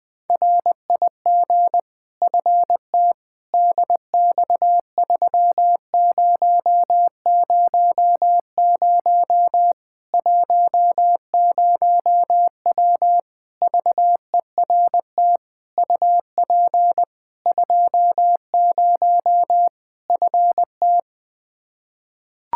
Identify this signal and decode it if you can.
Morse code, heard as a keyed tone, about 20 wpm, 700 Hz. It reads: RIG FT DX3000 10W VERT UP 20 FT
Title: Practice QSO - Exchange 2 - Info - 20wpm